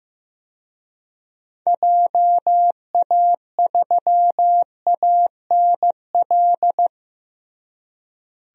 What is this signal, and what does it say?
Morse code, heard as a keyed tone, about 16 wpm, 700 Hz. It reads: JA3ANL
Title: DXCC-100 Call Signs - 6 Characters - Repeated 2x - 15wpm